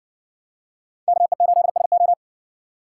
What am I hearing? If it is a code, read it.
Morse code, heard as a keyed tone, about 58 wpm, 700 Hz. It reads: KE8SO